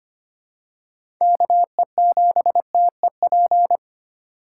Morse code, heard as a keyed tone, about 25 wpm, 700 Hz. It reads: KE7TEP
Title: USA Call Signs - 8x Character Spacing - 25wpm